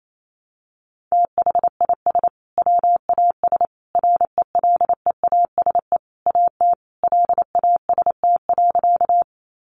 Morse code, heard as a keyed tone, about 28 wpm, 700 Hz. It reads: THIS WAS RELEASE AT LAST.
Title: Sentences from Top 1100 Words 28wpm